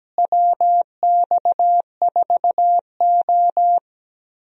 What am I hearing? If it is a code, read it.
Morse code, heard as a keyed tone, about 17 wpm, 700 Hz. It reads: WX4O
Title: USA Call Signs - 4 Characters - Repeated 2x - 17wpm